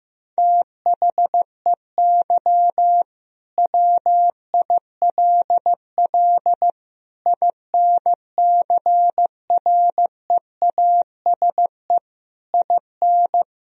Morse code, heard as a keyed tone, about 16 wpm, 700 Hz. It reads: THEY WILL INCREASE IN
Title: Sentences from Top 1000 Words - Review Part 4 - 15wpm